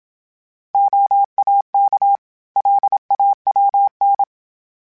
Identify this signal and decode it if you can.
Morse code, heard as a keyed tone, about 27 wpm, 800 Hz. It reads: OAK LAWN
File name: fromSMs / FromSMs mp3s / Cities@25